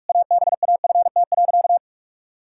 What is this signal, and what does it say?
Morse code, heard as a keyed tone, about 46 wpm, 700 Hz. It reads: ABAFT.
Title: Pangram Challenge v2 - 45wpm